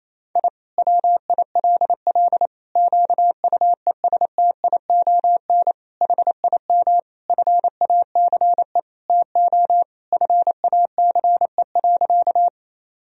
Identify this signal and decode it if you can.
Morse code, heard as a keyed tone, about 28 wpm, 700 Hz. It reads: I WILL QUESTION HIM FACE TO FACE.
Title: Sentences from Top 1000 Words - Review Part 2 - 28wpm